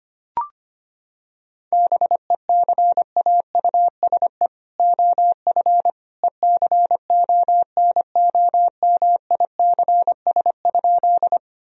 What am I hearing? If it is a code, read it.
Morse code, heard as a keyed tone, about 25 wpm, 700 Hz. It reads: BECAUSE OF ECONOMICS?
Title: Sentences from Top 1000 Words - Review Part 3 - 25wpm